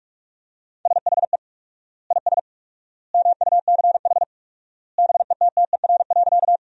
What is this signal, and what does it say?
Morse code, heard as a keyed tone, about 46 wpm, 700 Hz. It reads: SHE IS MUCH BETTER.
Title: 4x Word-Spacing - Sentences from Top 400 Words - 45wpm